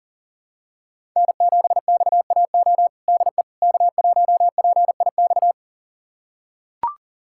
Morse code, heard as a keyed tone, about 37 wpm, 700 Hz. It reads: N7XAO DE K1PIX
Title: USA Callsign DE Callsign - 5 Characters - 40wpm